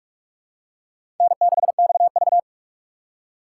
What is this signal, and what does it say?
Morse code, heard as a keyed tone, about 42 wpm, 700 Hz. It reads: N6XV